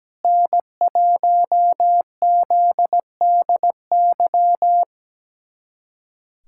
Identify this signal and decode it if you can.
Morse code, heard as a keyed tone, about 17 wpm, 700 Hz. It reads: N1ZDY
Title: USA Callsign DE Callsign - 5 Characters - 17wpm